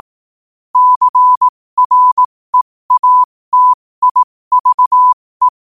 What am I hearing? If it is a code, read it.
Morse code, heard as a keyed tone, about 18 wpm, spaced as for 15 wpm, 1000 Hz. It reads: CREATIVE
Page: New quotes every day in morse code at 15 Words per minute.